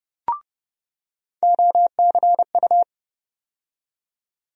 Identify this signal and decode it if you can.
Morse code, heard as a keyed tone, about 30 wpm, 700 Hz. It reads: OCU
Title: USA Call Sign Postfixes - 3 Letters 30wpm